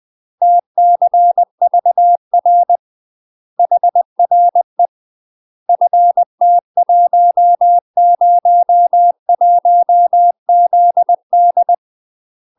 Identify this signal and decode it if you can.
Morse code, heard as a keyed tone, about 20 wpm, 700 Hz. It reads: TCVR HRE FT101ZD
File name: UK MASTER QSOs TRAINING / 20WPM / UK MASTER QSOs 0031